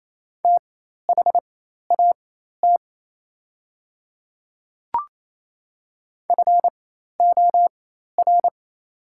Morse code, heard as a keyed tone, about 27 wpm, spaced as for 13 wpm, 690 Hz. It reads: THAT FOR
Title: 4x Character Spacing - Top 100 Words - 28wpm